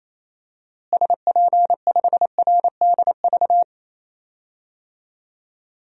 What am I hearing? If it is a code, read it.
Morse code, heard as a keyed tone, about 28 wpm, 700 Hz. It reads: SP5RDV
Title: DXCC-100 Call Signs - 6 Characters - Repeated 3x - 28wpm